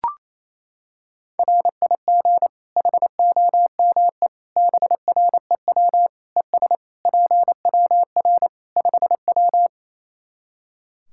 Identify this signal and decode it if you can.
Morse code, heard as a keyed tone, about 27 wpm, 700 Hz. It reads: RIG HOME BREW ES PWR 5W
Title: Practice QSO - Exchange 2 - Info - 28wpm